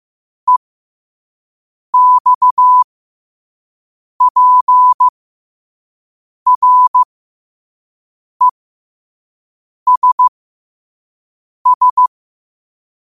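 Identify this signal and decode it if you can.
Morse code, heard as a keyed tone, about 15 wpm, spaced as for 5 wpm, 1000 Hz. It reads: EXPRESS
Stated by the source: Morse code practice at 5 wpm.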